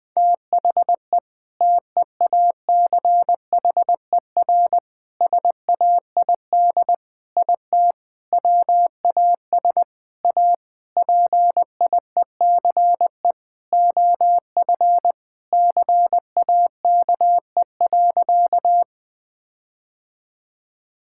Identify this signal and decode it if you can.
Morse code, heard as a keyed tone, about 20 wpm, 700 Hz. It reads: THE TEACHER SAID IT WAS A PIECE OF CAKE.
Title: All-Age Appropriate Jokes - 20wpm